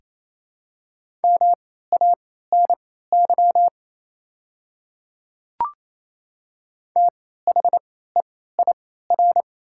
Morse code, heard as a keyed tone, about 28 wpm, spaced as for 16 wpm, 700 Hz. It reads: MANY THEIR
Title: 3x Character Spacing - Top 100 Words - 28wpm